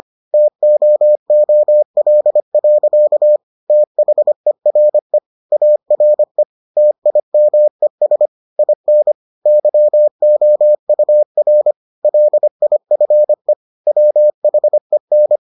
Morse code, heard as a keyed tone, about 25 wpm, 600 Hz. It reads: TOOL. THERE ARE TIMES IN YOUR LIFE WHEN